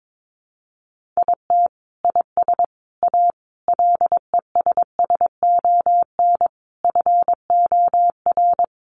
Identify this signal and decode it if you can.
Morse code, heard as a keyed tone, about 22 wpm, 700 Hz. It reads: IT IS A LESSON FOR
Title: Sentences from Top 1200 Words 22wpm